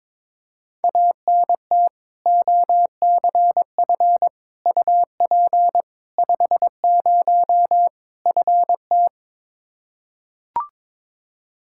Morse code, heard as a keyed tone, about 22 wpm, 700 Hz. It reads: ANT OCF UP 50 FT
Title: Practice QSO - Exchange 2 - Info - 22wpm